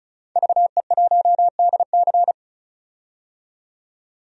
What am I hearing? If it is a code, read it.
Morse code, heard as a keyed tone, about 36 wpm, 700 Hz. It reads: VE1DC
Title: DXCC-100 Call Signs - 5 Characters - 35wpm